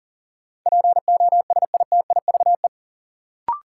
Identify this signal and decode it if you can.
Morse code, heard as a keyed tone, about 40 wpm, 700 Hz. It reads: POSITIVE
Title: Sentences from Top 1000 Words - Review Part 5 - New Word Drill - 40wpm